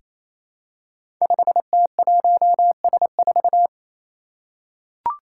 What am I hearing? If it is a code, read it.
Morse code, heard as a keyed tone, about 27 wpm, 700 Hz. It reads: 5T1S4
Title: Five Challenging Letter-Number Combinations 28wpm